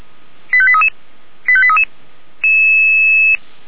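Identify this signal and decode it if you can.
Five-tone selective call (ZVEI), heard as quick runs of five tones, tones of about 70 ms. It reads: 85833 85833
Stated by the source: ZVEI five-tone selective call received via SDR.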